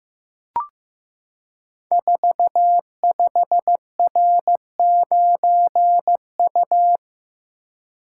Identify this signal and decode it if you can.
Morse code, heard as a keyed tone, about 15 wpm, 700 Hz. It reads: 45R9U
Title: Code Groups - 5 Letter:Numbers - 15wpm